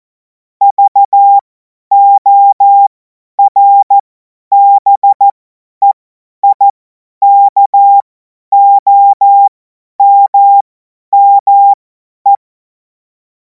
Morse code, heard as a keyed tone, about 14 wpm, spaced as for 10 wpm, 800 Hz. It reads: VORBEIKOMME
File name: Morsekurs / Witze / Witz 10. Februar 2026 50 F70